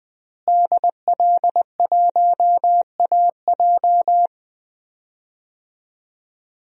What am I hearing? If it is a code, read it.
Morse code, heard as a keyed tone, about 20 wpm, 700 Hz. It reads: DL1AJ
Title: DXCC-100 Call Signs - 5 Characters - Repeated 3x - 20wpm